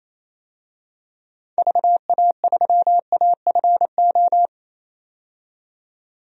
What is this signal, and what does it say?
Morse code, heard as a keyed tone, about 28 wpm, 700 Hz. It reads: VA3AFO
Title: DXCC-100 Call Signs - 6 Characters - 28wpm